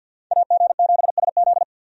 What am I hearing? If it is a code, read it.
Morse code, heard as a keyed tone, about 52 wpm, 700 Hz. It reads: AG7SZ
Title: USA Call Signs - 5 Characters - 50wpm